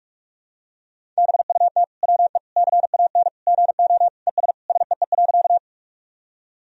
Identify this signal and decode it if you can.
Morse code, heard as a keyed tone, about 41 wpm, 700 Hz. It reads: BUT WE CAN GO ES SEE.
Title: Sentences from Top 100 Words and Abbreviations 45wpm